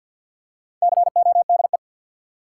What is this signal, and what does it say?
Morse code, heard as a keyed tone, about 47 wpm, 700 Hz. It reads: CODE
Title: Four Letter Words - 50wpm